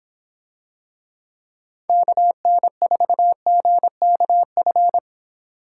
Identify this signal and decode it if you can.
Morse code, heard as a keyed tone, about 26 wpm, 700 Hz. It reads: KN4GKF